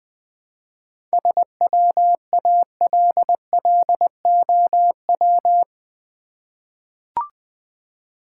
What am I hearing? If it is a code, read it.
Morse code, heard as a keyed tone, about 20 wpm, 700 Hz. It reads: SWALLOW
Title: 7 Letter Words - First 3 Letters Given - 20wpm